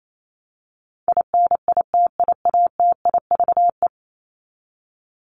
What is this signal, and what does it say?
Morse code, heard as a keyed tone, about 28 wpm, 700 Hz. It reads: INITIATIVE
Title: Top 1751-2000 Words 28wpm